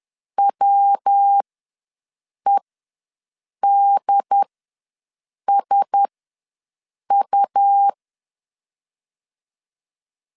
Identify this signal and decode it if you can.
Morse code, heard as a keyed tone, about 10 wpm, spaced as for 6 wpm, 800 Hz. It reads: WEDSU
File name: LEZIONI / 5 LBVC / 5 L ws 10wpm 3636